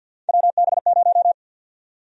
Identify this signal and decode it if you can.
Morse code, heard as a keyed tone, about 52 wpm, 700 Hz. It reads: WB0
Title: USA Call Sign Prefixes - Repeated 2x - 50wpm